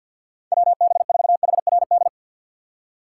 Morse code, heard as a keyed tone, about 46 wpm, 700 Hz. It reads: WB4HRD
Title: USA Call Signs - 6 Characters - 6x Character Spacing - 50wpm